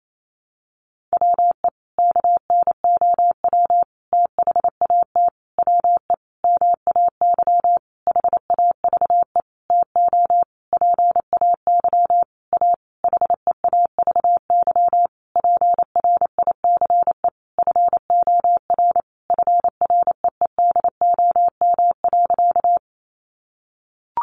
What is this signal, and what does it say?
Morse code, heard as a keyed tone, about 28 wpm, 700 Hz. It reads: WE KNOW THAT WE MAY HAVE TO PAY A HEAVY PRICE FOR FREEDOM.
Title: Sentences from Top 1100 Words 28wpm